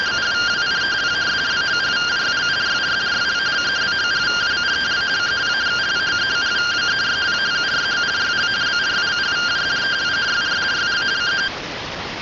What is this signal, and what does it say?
Two-tone teleprinter signal, about 170 Hz shift.